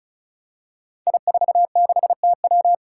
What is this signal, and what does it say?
Morse code, heard as a keyed tone, about 35 wpm, 700 Hz. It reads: I46TW
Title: Five Challenging Letter-Number Combinations 35wpm